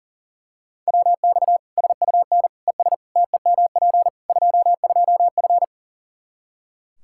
Morse code, heard as a keyed tone, about 40 wpm, 700 Hz. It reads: WX SUN ES TEMP 22F
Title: Practice QSO - Exchange 2 - Info - 40wpm